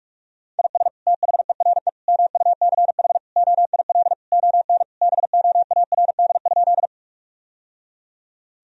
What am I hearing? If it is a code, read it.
Morse code, heard as a keyed tone, about 46 wpm, 700 Hz. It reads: IS THERE MUCH OIL ON BOARD?
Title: Sentences from Top 1000 Words - Review Part 4 - 45wpm